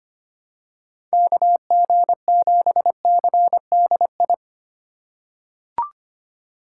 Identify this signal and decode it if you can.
Morse code, heard as a keyed tone, about 25 wpm, 700 Hz. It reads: KG7CDI
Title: USA Call Signs - 6 Characters - 2x Character Spacing - 25wpm